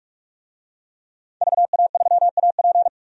Morse code, heard as a keyed tone, about 46 wpm, 700 Hz. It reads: VA3AP
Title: DXCC-100 Call Signs - 5 Characters - 45wpm